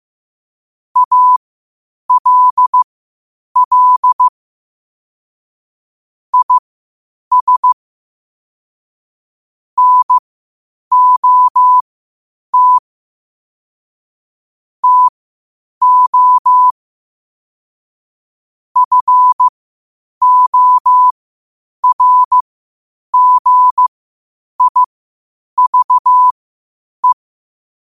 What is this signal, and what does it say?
Morse code, heard as a keyed tone, about 15 wpm, spaced as for 8 wpm, 1000 Hz. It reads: ALL IS NOT TO FORGIVE
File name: qotdfiles / 8wpm